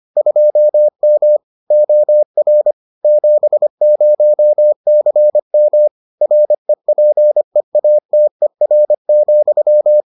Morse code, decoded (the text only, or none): 2M OR 70CM REPEATER,